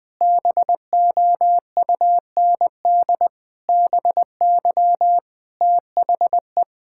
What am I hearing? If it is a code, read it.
Morse code, heard as a keyed tone, about 20 wpm, 700 Hz. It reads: BOUND BY THE
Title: Sets of 9 Words - 20wpm